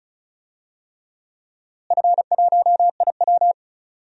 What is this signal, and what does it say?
Morse code, heard as a keyed tone, about 36 wpm, 690 Hz. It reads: F1IW
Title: DXCC-100 Call Signs - 4 Characters - Repeated 3x - 35wpm